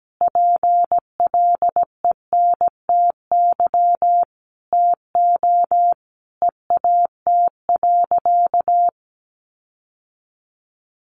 Morse code, heard as a keyed tone, about 17 wpm, 700 Hz. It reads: PLENTY TO EAT.
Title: Sentences from Top 1300 Words 17wpm